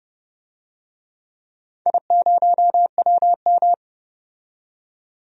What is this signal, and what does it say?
Morse code, heard as a keyed tone, about 29 wpm, 700 Hz. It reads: I0WM
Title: DXCC-100 Call Signs - 4 Characters - 30wpm